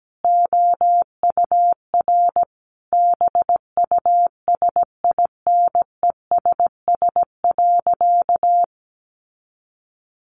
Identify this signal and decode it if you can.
Morse code, heard as a keyed tone, about 17 wpm, 700 Hz. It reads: OUR BUSINESS.
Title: Sentences from Top 1400 Words 17wpm